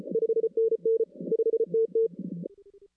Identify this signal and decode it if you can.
Morse code, heard as a keyed tone, about 29 wpm, 450 Hz. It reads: 5NN 5TT
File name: 2016-ARRL-DX-CW-K5TR